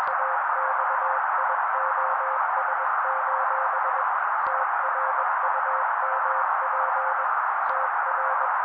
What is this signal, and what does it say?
Morse code, heard as a keyed tone, about 21 wpm, 530 Hz. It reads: AXIOS: TRUMP TR
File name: daily / 20wpm noise